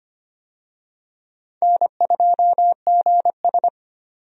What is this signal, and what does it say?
Morse code, heard as a keyed tone, about 25 wpm, 700 Hz. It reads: N2GS
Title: USA Call Signs - 4 Characters - 25wpm